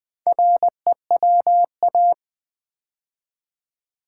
Morse code, heard as a keyed tone, about 20 wpm, 700 Hz. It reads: REWA
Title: Build A Word - 2 Letters At A Time - 6-Letter Words - 20wpm